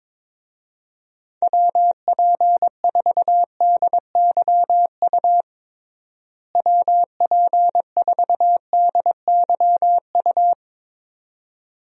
Morse code, heard as a keyed tone, about 22 wpm, 700 Hz. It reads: WP4DYU WP4DYU
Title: USA Call Signs - 6 Characters - Repeated 2x - 22wpm